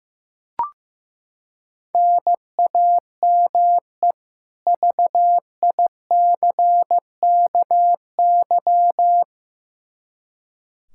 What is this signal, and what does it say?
Morse code, heard as a keyed tone, about 15 wpm, 700 Hz. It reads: NAME VICKY
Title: Practice QSO - Exchange 1 - Beginning - 15wpm